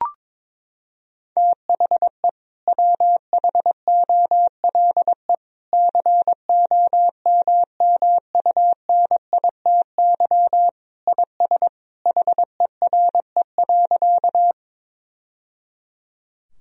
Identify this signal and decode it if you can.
Morse code, heard as a keyed tone, about 22 wpm, 700 Hz. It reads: THE WHOLE COMMUNITY IS HERE.
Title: Sentences from Top 1000 Words - Review Part 4 - 22wpm